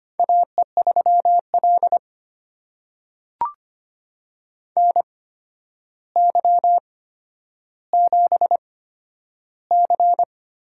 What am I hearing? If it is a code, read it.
Morse code, heard as a keyed tone, about 25 wpm, spaced as for 11 wpm, 700 Hz. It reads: AE3L NY7C
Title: USA Call Signs - 4 Characters - 8x Character Spacing - 25wpm